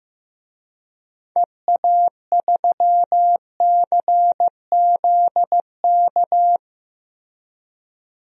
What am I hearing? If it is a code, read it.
Morse code, heard as a keyed tone, about 15 wpm, 700 Hz. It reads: EA3CZK
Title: DXCC-100 Call Signs - 6 Characters - Repeated 2x - 15wpm